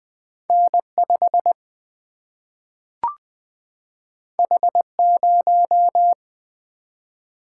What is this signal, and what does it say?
Morse code, heard as a keyed tone, about 20 wpm, 700 Hz. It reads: N5 H0
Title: Single-Letter Single-Number 20wpm